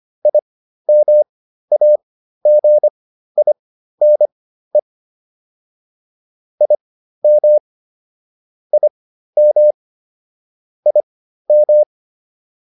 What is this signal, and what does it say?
Morse code, heard as a keyed tone, about 25 wpm, spaced as for 13 wpm, 600 Hz. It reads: IMAGINE IM IM IM
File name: IM7 13